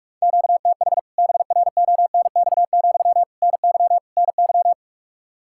Morse code, heard as a keyed tone, about 46 wpm, 700 Hz. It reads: QTH BRONX, NY NY
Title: K1USN SST QSO Practice 45wpm